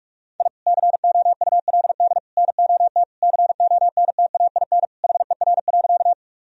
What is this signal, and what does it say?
Morse code, heard as a keyed tone, about 48 wpm, 700 Hz. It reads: I COULD NOT CONTAIN HER.
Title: Sentences from Top 1000 Words - Review Part 2 - 45wpm